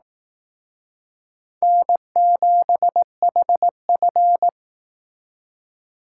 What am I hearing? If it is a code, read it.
Morse code, heard as a keyed tone, about 18 wpm, 700 Hz. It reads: N7HF
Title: Speed Racing - USA 1x2 Call Signs 15wpm